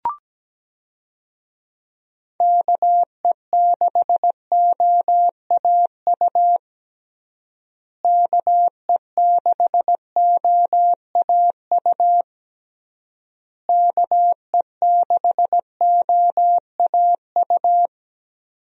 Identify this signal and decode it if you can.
Morse code, heard as a keyed tone, about 17 wpm, 700 Hz. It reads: KE6OAU KE6OAU KE6OAU
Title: USA Call Signs - 6 Characters - Repeated 3x - 17wpm